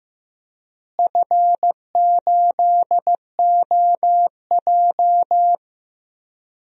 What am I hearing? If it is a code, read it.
Morse code, heard as a keyed tone, about 15 wpm, 700 Hz. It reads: F8OJ
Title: DXCC-100 Call Signs - 4 Characters - Repeated 3x - 15wpm